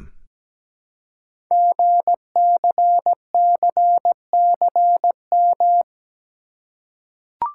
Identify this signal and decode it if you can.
Morse code, heard as a keyed tone, about 17 wpm, 700 Hz. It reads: GCCCM
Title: Code Groups - 5 Letters - 17wpm